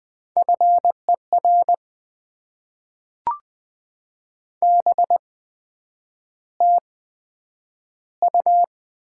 Morse code, heard as a keyed tone, about 20 wpm, 700 Hz. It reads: F E R B T U
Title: Top 20 QSO Elements - 8x Character Spacing - 20wpm